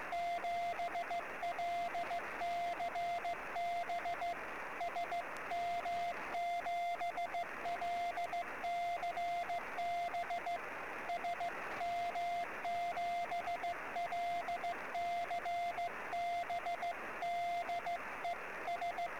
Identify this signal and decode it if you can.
Morse code, heard as a keyed tone, about 16 wpm, 690 Hz, with radio noise. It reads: 7LCB SM7LCB SM7LCB DE S
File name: sm7lcb / 08 sm5le-p jo98ax 23cm